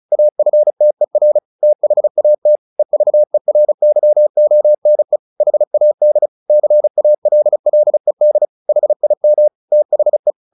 Morse code, heard as a keyed tone, about 34 wpm, 600 Hz. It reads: AFTER THAT EVERYONE HAD CALLED HIM THE
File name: the old man and the sea 2 35wpm